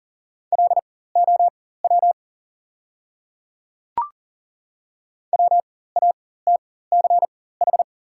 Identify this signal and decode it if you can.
Morse code, heard as a keyed tone, about 40 wpm, spaced as for 19 wpm, 700 Hz. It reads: LOW WATCH
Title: 4x Character Spacing - Top 300 Words - 40wpm